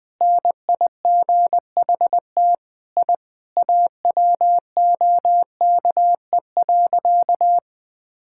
Morse code, heard as a keyed tone, about 20 wpm, 700 Hz. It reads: NIGHT I AWOKE.
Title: Sentences with 4-Word Phrases - 20wpm